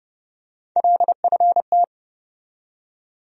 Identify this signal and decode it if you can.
Morse code, heard as a keyed tone, about 30 wpm, 700 Hz. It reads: LFT
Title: USA Call Sign Postfixes - 3 Letters 30wpm